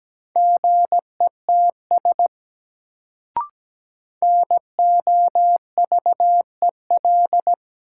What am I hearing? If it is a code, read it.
Morse code, heard as a keyed tone, about 17 wpm, 700 Hz. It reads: GETS NOVEL
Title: Sentences from Top 1300 Words - New Word Drill Alt - 17wpm